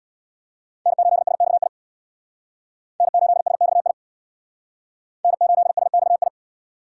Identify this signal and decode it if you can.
Morse code, heard as a keyed tone, about 62 wpm, 700 Hz. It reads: N8SXI N8SXI N8SXI
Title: Speed Racing - USA Call Signs 50wpm